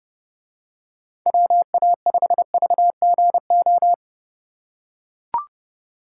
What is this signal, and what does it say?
Morse code, heard as a keyed tone, about 29 wpm, 700 Hz. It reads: WA5VGO